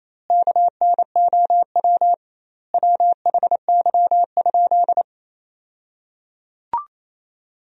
Morse code, heard as a keyed tone, about 28 wpm, 700 Hz. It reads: KNOW WHY?
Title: 2x Word-Spacing - Sentences from Top 200 Words - 28wpm